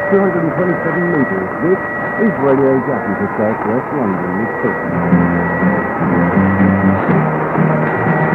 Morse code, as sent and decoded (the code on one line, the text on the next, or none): none